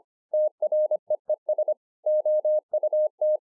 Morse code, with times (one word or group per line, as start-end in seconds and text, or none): 0.3-1.7 TREES
2.1-3.4 OUT